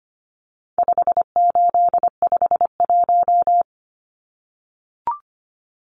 Morse code, heard as a keyed tone, about 25 wpm, 700 Hz. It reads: 5851
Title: Numbers - 4-Digit 25wpm